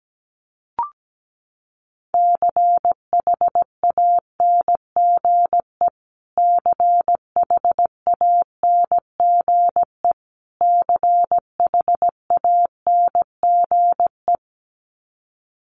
Morse code, heard as a keyed tone, about 17 wpm, 700 Hz. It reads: CHANGE CHANGE CHANGE
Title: Top 200 Words Repeated 17wpm